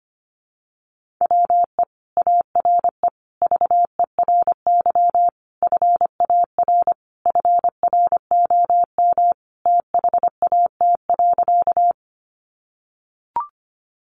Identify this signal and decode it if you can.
Morse code, heard as a keyed tone, about 25 wpm, 700 Hz. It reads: WE ARE VERY FAR FROM THAT.
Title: Sentences from Top 300 Words 25wpm